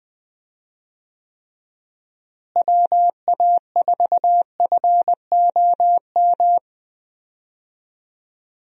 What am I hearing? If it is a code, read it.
Morse code, heard as a keyed tone, about 20 wpm, 700 Hz. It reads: WA4FOM
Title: USA Call Signs - 6 Characters - Repeated 3x - 20wpm